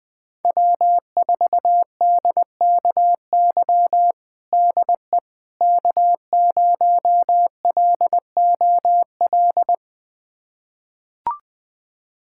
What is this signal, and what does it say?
Morse code, heard as a keyed tone, about 20 wpm, 700 Hz. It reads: W4DKY DE K0LOL
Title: USA Callsign DE Callsign - 5 Characters - 20wpm